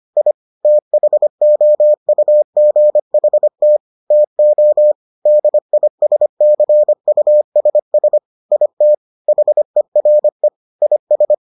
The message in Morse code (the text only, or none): I THOUGHT TO DISCUSS IT HERE IS